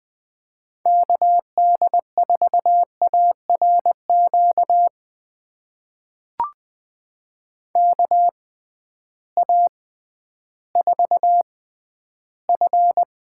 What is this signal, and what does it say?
Morse code, heard as a keyed tone, about 20 wpm, spaced as for 12 wpm, 700 Hz. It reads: KD4ARQ KA4F
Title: USA Call Signs - 6 Characters - 6x Character Spacing - 20wpm